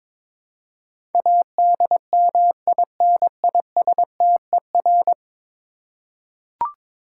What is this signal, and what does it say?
Morse code, heard as a keyed tone, about 22 wpm, 700 Hz. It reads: ADMINISTER
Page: morse-code-ninja-practice-22wpm